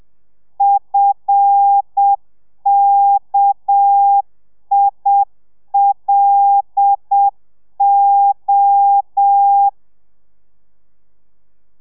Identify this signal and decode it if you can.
Morse code, heard as a keyed tone, about 7 wpm, 800 Hz. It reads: FKILO